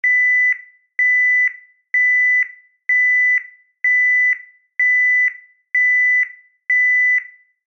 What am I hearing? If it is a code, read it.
There is no code.